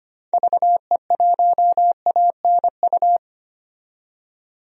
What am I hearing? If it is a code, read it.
Morse code, heard as a keyed tone, about 25 wpm, 700 Hz. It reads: VE1ANU
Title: DXCC-100 Call Signs - 6 Characters - 25wpm